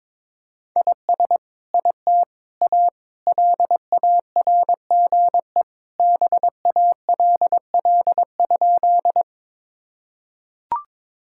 Morse code, heard as a keyed tone, about 22 wpm, 700 Hz. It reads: IS IT A LARGE BALL?
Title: Sentences from Top 1000 Words - Review Part 6 - 22wpm